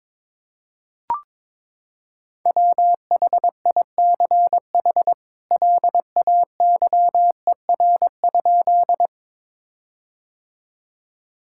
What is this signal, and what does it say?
Morse code, heard as a keyed tone, about 21 wpm, 700 Hz. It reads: WHICH LAYER?
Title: Sentences from Top 1600 Words 22wpm